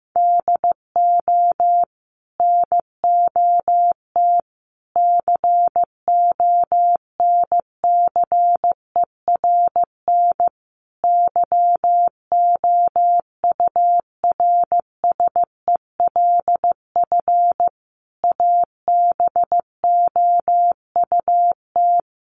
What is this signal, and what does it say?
Morse code, heard as a keyed tone, about 15 wpm, 700 Hz. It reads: DO NOT CONCERN YOURSELF ABOUT
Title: Sentences from Top 1000 Words 15wpm